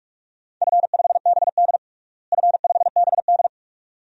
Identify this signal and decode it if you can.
Morse code, heard as a keyed tone, about 46 wpm, 700 Hz. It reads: F5BD F5BD
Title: DXCC-100 Call Signs - 4 Characters - Repeated 2x - 45wpm